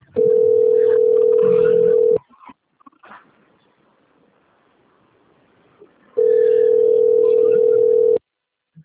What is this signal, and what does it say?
Ringback tone (North America), 440 + 480 Hz.
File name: rg-1600-unknown- - - .38802